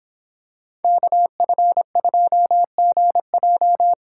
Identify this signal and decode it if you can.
Morse code, heard as a keyed tone, about 26 wpm, 700 Hz. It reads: KF2GJ